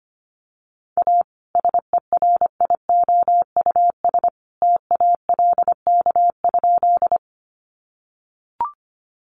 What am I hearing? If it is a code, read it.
Morse code, heard as a keyed tone, about 25 wpm, 700 Hz. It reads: A SERIOUS TALK?
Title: Sentences from Top 1100 Words 25wpm